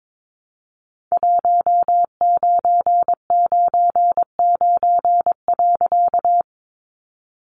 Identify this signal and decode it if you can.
Morse code, heard as a keyed tone, about 22 wpm, 700 Hz. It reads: 1999.
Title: A Sound Of Thunder 22wpm